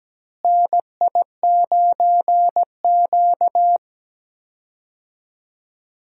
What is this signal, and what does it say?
Morse code, heard as a keyed tone, about 17 wpm, 700 Hz. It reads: NI9Q
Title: USA Call Signs - 4 Characters - Repeated 2x - 17wpm